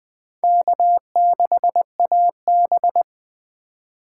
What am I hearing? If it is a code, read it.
Morse code, heard as a keyed tone, about 19 wpm, 700 Hz. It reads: K6AB